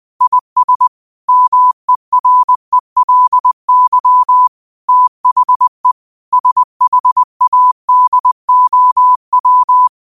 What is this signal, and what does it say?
Morse code, heard as a keyed tone, about 20 wpm, 1000 Hz. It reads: IS MERELY THE SHADOW
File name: qotdfiles / 20wpm